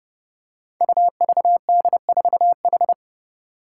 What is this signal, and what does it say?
Morse code, heard as a keyed tone, about 30 wpm, 700 Hz. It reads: UVD4H
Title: Five Challenging Letter-Number Combinations 30wpm